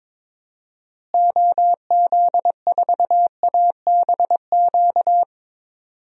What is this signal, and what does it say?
Morse code, heard as a keyed tone, about 22 wpm, 700 Hz. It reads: OZ4ABQ
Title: DXCC-100 Call Signs - 6 Characters - Repeated 2x - 22wpm